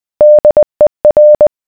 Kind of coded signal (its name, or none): Morse code